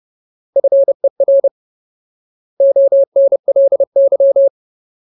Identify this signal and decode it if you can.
Morse code, heard as a keyed tone, about 30 wpm, 550 Hz. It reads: FER ONLY
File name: Ultimate Word List / 30WPM Ultimate